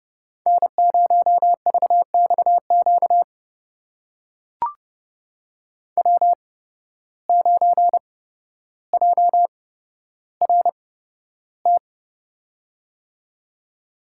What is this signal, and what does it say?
Morse code, heard as a keyed tone, about 30 wpm, spaced as for 13 wpm, 700 Hz. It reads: N0VXQ W9JRT
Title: USA Call Signs - 5 Characters - 8x Character Spacing - 30wpm